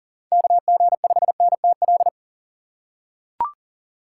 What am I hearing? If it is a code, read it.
Morse code, heard as a keyed tone, about 40 wpm, 700 Hz. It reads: KG5NTL
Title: USA Call Signs - 6 Characters - 6x Character Spacing - 40wpm